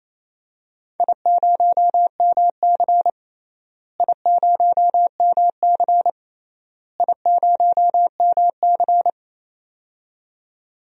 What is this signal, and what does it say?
Morse code, heard as a keyed tone, about 28 wpm, 700 Hz. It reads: I0MC I0MC I0MC
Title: DXCC-100 Call Signs - 4 Characters - Repeated 3x - 28wpm